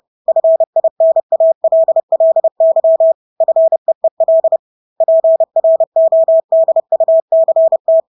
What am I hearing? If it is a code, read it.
Morse code, heard as a keyed tone, about 30 wpm, 650 Hz. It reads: FINALLY FEEL PRODUCT
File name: Engelske ord 30-30 wpm